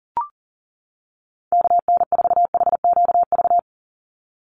Morse code, heard as a keyed tone, about 40 wpm, 700 Hz. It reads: KN4HQV
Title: USA Call Signs 40wpm